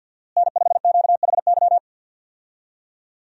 Morse code, heard as a keyed tone, about 51 wpm, 700 Hz. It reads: N5QHY
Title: USA Call Signs - 5 Characters - Repeated 3x - 50wpm